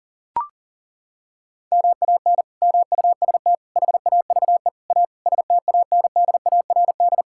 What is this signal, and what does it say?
Morse code, heard as a keyed tone, about 40 wpm, 700 Hz. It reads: MAN MUST HAVE A STANDARD
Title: Sentences from Top 1000 Words - Review Part 6 - 40wpm